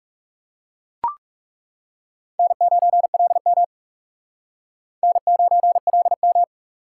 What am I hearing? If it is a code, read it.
Morse code, heard as a keyed tone, about 38 wpm, 700 Hz. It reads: N9LM N9LM